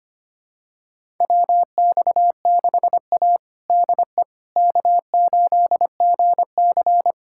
Morse code, heard as a keyed tone, about 25 wpm, 700 Hz. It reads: WX6A DE K8GC
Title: USA Callsign DE Callsign - 4 Characters - 25wpm